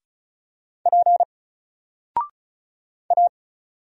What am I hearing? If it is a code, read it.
Morse code, heard as a keyed tone, about 36 wpm, 700 Hz. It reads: P A
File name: Single-Letter-Number-35wpm